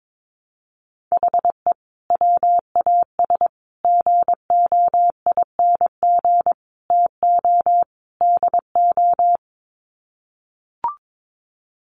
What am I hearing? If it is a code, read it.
Morse code, heard as a keyed tone, about 22 wpm, 700 Hz. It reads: HE WAS GOING TO DO
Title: Sets of 5 Words Encore2 22wpm